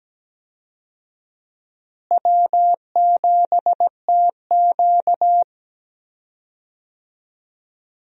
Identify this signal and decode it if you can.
Morse code, heard as a keyed tone, about 17 wpm, 700 Hz. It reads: W7TQ
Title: USA Call Signs - 4 Characters - Repeated 3x - 17wpm